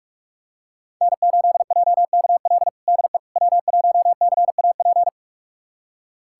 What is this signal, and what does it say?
Morse code, heard as a keyed tone, about 45 wpm, 700 Hz. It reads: N8JKL DE W1CAP
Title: USA Callsign DE Callsign - 5 Characters - 45wpm